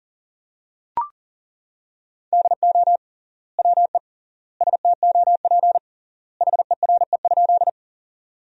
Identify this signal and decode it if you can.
Morse code, heard as a keyed tone, about 37 wpm, 700 Hz. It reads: DO WE STOP HERE?
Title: 3x Word-Spacing - Sentences from Top 300 Words - 40wpm